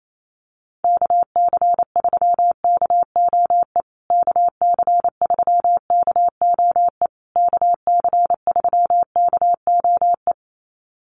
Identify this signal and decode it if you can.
Morse code, heard as a keyed tone, about 28 wpm, 700 Hz. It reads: KC3KOE KC3KOE KC3KOE
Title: USA Call Signs Repeated 3x 28wpm